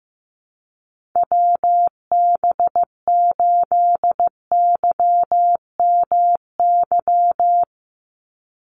Morse code, heard as a keyed tone, about 15 wpm, 700 Hz. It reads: WB8YMY
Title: USA Call Signs 15wpm